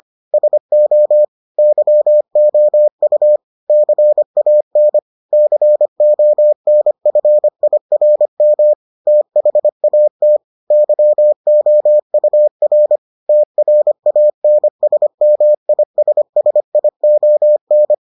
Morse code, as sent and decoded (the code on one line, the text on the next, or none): ... --- / -.-- --- ..- / -.-. .- -. / -.-. --- -. ..-. .. .-. -- / - .... .- - / -.-- --- ..- .-. / - .-. .- -. ... -- .. ... ... .. --- -.
SO YOU CAN CONFIRM THAT YOUR TRANSMISSION